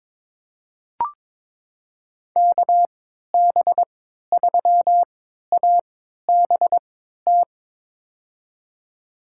Morse code, heard as a keyed tone, about 22 wpm, spaced as for 13 wpm, 700 Hz. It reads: KB3ABT
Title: USA Call Signs - 3x Character Spacing - 22wpm